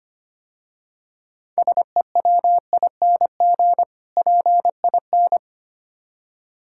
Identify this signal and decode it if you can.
Morse code, heard as a keyed tone, about 25 wpm, 700 Hz. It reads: SEWING PIN
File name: practice / 25wpm / Difficult Words - Speed-Racing with Hints - 25wpm